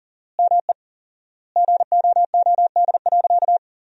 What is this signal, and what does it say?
Morse code, heard as a keyed tone, about 40 wpm, 700 Hz. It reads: ME GOOD.
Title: 4x Word-Spacing - Sentences from Top 200 Words - 40wpm